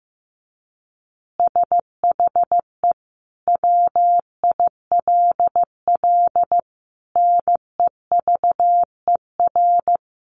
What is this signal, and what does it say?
Morse code, heard as a keyed tone, about 15 wpm, 700 Hz. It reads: SHE WILL NEVER
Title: Additional Easy Phrases 15wpm